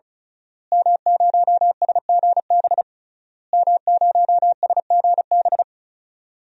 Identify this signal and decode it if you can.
Morse code, heard as a keyed tone, about 34 wpm, 700 Hz. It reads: M0SGB M0SGB
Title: DXCC-100 Call Signs - 5 Characters - Repeated 3x - 35wpm